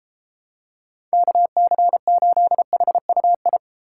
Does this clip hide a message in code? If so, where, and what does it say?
Yes: Morse code from 1.1 to 3.6 s, heard as a keyed tone, about 32 wpm, 700 Hz. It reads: KC8HUI